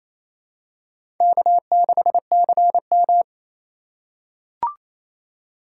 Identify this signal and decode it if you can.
Morse code, heard as a keyed tone, about 28 wpm, 700 Hz. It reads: K6CM
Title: USA Call Signs - 4 Characters - 3x Character Spacing - 28wpm